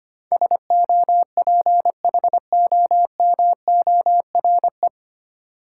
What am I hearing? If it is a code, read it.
Morse code, heard as a keyed tone, about 25 wpm, 700 Hz. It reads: SOPHOMORE
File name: practice / 25wpm / 9 Letter Words - First 3 Letters Given - 25wpm